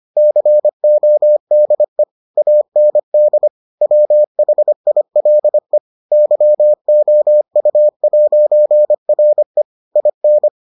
Morse code, heard as a keyed tone, about 25 wpm, 600 Hz. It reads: CODE AND WHILE YOU'RE IN